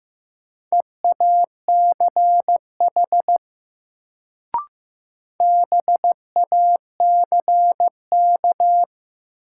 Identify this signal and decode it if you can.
Morse code, heard as a keyed tone, about 15 wpm, 700 Hz. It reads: EACH BACK
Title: Sentences from Top 200 Words - New Word Drill - 15wpm